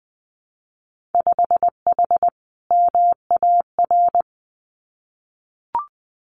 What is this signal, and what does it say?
Morse code, heard as a keyed tone, about 20 wpm, 700 Hz. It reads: 5H MAR
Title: Winter Field Day Exchanges Encore 20wpm